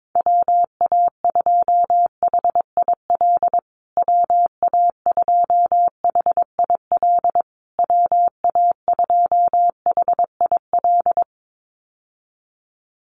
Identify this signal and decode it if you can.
Morse code, heard as a keyed tone, about 22 wpm, 700 Hz. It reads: WA2HIL WA2HIL WA2HIL
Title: USA Call Signs Repeated 3x 22wpm